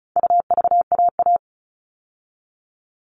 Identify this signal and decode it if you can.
Morse code, heard as a keyed tone, about 34 wpm, 700 Hz. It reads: UVAA
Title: All Cut Numbers - 4-Digit 35wpm